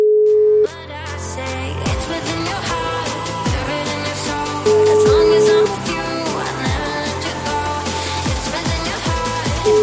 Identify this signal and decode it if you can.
Ringback tone (Europe, 425 Hz), with noise.